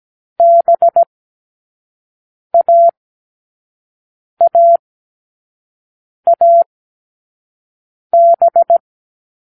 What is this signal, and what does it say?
Morse code, heard as a keyed tone, about 17 wpm, spaced as for 5 wpm, 680 Hz. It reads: BAAAB